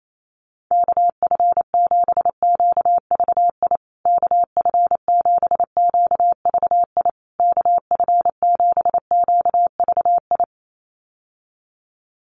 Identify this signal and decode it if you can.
Morse code, heard as a keyed tone, about 28 wpm, 700 Hz. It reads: KF7QVI KF7QVI KF7QVI
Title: USA Call Signs Repeated 3x 28wpm